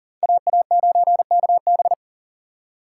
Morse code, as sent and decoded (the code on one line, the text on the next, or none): .- .- ----. -.- -...
AA9KB